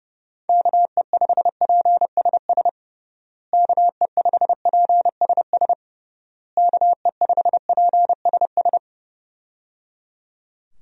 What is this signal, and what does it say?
Morse code, heard as a keyed tone, about 29 wpm, 700 Hz. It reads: KE5PSS KE5PSS KE5PSS
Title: USA Call Signs - 6 Characters - Repeated 3x - 30wpm